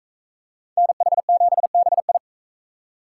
Morse code, heard as a keyed tone, about 42 wpm, 700 Hz. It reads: NH7BI